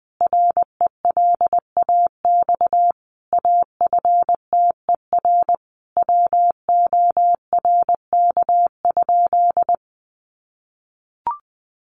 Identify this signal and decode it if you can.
Morse code, heard as a keyed tone, about 20 wpm, 700 Hz. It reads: RELAX AFTER WORK?
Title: Sentences from Top 1400 Words 20wpm